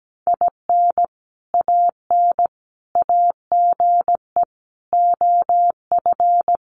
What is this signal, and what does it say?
Morse code, heard as a keyed tone, about 17 wpm, 700 Hz. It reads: IN AN AGE OF
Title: Sentences from Top 1200 Words 17wpm